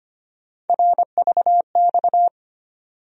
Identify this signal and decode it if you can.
Morse code, heard as a keyed tone, about 25 wpm, 700 Hz. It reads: RVX
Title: USA Call Sign Postfixes - 3 Letters 25wpm